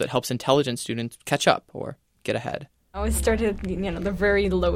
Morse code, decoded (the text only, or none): none